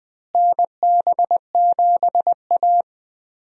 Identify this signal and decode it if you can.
Morse code, heard as a keyed tone, about 20 wpm, 700 Hz. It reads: NB7A
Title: USA Call Signs - 4 Characters - Repeated 2x - 20wpm